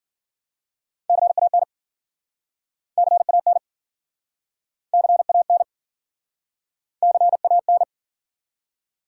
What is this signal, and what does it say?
Morse code, heard as a keyed tone, about 55 wpm, 700 Hz. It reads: CAN CAN CAN CAN